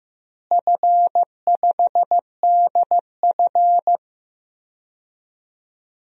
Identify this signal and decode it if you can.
Morse code, heard as a keyed tone, about 15 wpm, 700 Hz. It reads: F5DF
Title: DXCC-100 Call Signs - 4 Characters - Repeated 3x - 15wpm